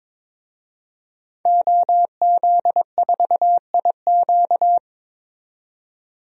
Morse code, heard as a keyed tone, about 22 wpm, 700 Hz. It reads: OZ4IQ
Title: DXCC-100 Call Signs - 5 Characters - 22wpm